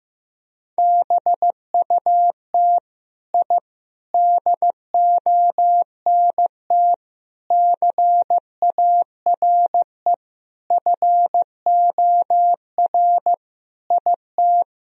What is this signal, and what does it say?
Morse code, heard as a keyed tone, about 15 wpm, 700 Hz. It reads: BUT I DONT CARE FOR IT
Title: Sentences from Top 1000 Words - Review Part 1 - 15wpm